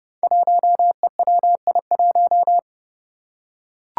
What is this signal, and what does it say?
Morse code, heard as a keyed tone, about 30 wpm, 700 Hz. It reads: 1EWI1
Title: Five Challenging Letter-Number Combinations 30wpm